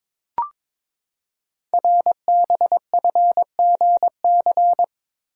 Morse code, heard as a keyed tone, about 22 wpm, 700 Hz. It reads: RBFGC
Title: Code Groups - 5 Letters - 22wpm